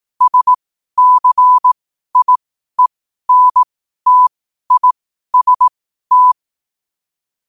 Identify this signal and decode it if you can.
Morse code, heard as a keyed tone, about 18 wpm, spaced as for 12 wpm, 1000 Hz. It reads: SCIENTIST
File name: qotdfiles / 12wpm